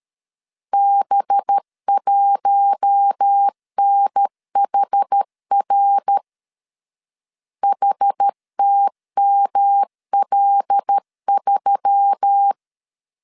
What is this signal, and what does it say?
Morse code, heard as a keyed tone, about 13 wpm, 800 Hz. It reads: B1NHR HTML3
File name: LEZIONI / 6 PKX0 / 6 P 12wpm 2033